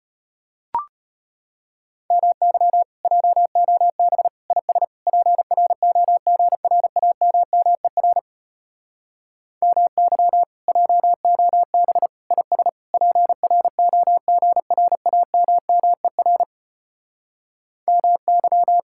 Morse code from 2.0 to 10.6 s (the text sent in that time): MY JOB IS PROGRAMMER MY